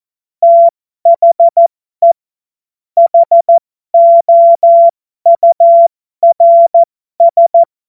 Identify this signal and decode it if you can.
Morse code, heard as a keyed tone, about 14 wpm, spaced as for 12 wpm, 680 Hz. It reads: THE HOURS
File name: Morsekurs / Witze